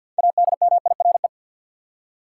Morse code, heard as a keyed tone, about 50 wpm, 700 Hz. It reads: ADMIRE
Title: Six Letter Words - 50wpm